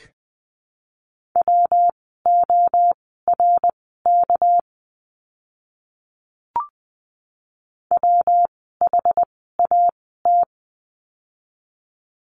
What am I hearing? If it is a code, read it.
Morse code, heard as a keyed tone, about 20 wpm, spaced as for 14 wpm, 700 Hz. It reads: WORK WHAT
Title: 230 Lesson 37f - Letter x - Words 20wpm